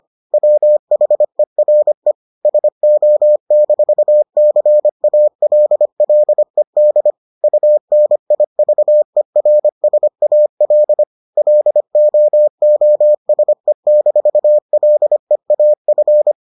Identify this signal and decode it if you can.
Morse code, heard as a keyed tone, about 25 wpm, 600 Hz. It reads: WHERE SO-CALLED UNIVERSAL LOOSE-LEAF